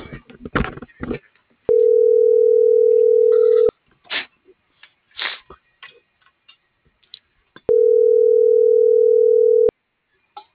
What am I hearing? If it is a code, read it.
Ringback tone (North America), 440 + 480 Hz.